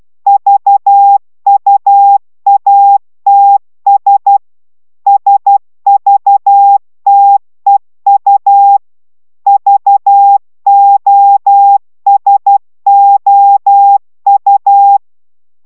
Morse code, heard as a keyed tone, about 12 wpm, 800 Hz. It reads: VUATS SVTEU VOSOU